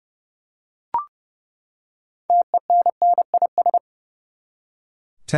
Morse code, heard as a keyed tone, about 30 wpm, 700 Hz. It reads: TENNIS
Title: Six Letter Words - 30wpm